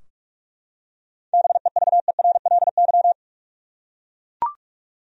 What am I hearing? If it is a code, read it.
Morse code, heard as a keyed tone, about 44 wpm, 700 Hz. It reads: BEVERLY
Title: USA Common Names - 45wpm